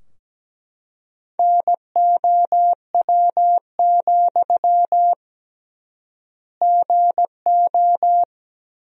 Morse code, heard as a keyed tone, about 17 wpm, 700 Hz. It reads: NOW, GO
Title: 3x Word-Spacing - Sentences from Top 200 Words - 17wpm